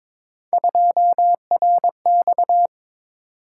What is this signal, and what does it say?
Morse code, heard as a keyed tone, about 23 wpm, 700 Hz. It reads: 2RX
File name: practice / 22wpm / Three Letter-Number Combinations - 22wpm